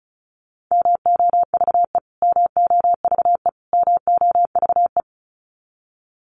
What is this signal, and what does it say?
Morse code, heard as a keyed tone, about 34 wpm, 700 Hz. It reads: MOVE MOVE MOVE
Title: Top 300 Words Repeated 35wpm